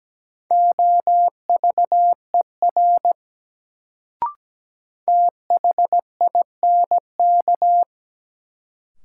Morse code, heard as a keyed tone, about 17 wpm, 700 Hz. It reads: OVER THINK
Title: Sets of 2 Words - New Word Drill - 17wpm